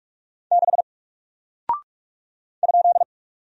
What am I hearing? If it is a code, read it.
Morse code, heard as a keyed tone, about 44 wpm, 700 Hz. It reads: 6 ?
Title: Single Character (Letter:Number:Punctuation:Prosign) Alt 45wpm